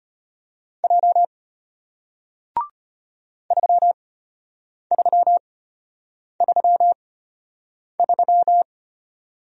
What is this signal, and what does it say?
Morse code, heard as a keyed tone, about 33 wpm, spaced as for 8 wpm, 700 Hz. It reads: J 3333